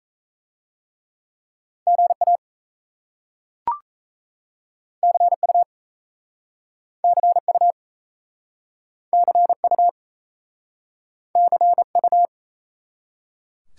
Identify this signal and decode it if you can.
Morse code, heard as a keyed tone, about 38 wpm, 700 Hz. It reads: GA CU CU CU CU